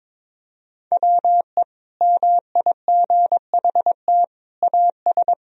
Morse code, heard as a keyed tone, about 22 wpm, 700 Hz. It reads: WE MIGHT AS
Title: Sentences from Top 1000 Words - Review Part 1 - 22wpm